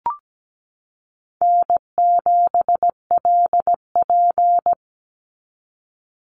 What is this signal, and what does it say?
Morse code, heard as a keyed tone, about 17 wpm, 700 Hz. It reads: N7LP
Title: NA SOTA Chaser Callsigns 17wpm